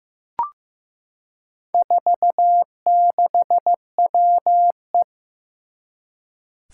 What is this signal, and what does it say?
Morse code, heard as a keyed tone, about 15 wpm, 700 Hz. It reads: 46WE
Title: Four Challenging Letter-Number Combinations 15wpm